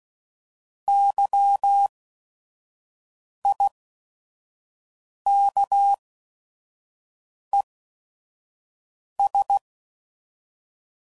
Morse code, heard as a keyed tone, about 16 wpm, spaced as for 5 wpm, 800 Hz. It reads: YIKES